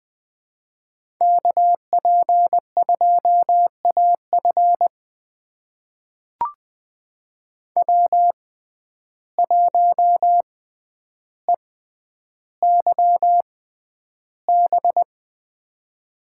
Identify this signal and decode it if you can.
Morse code, heard as a keyed tone, about 20 wpm, spaced as for 10 wpm, 700 Hz. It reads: KP2AF W1EYB
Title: USA Call Signs - 5 Characters - 6x Character Spacing - 20wpm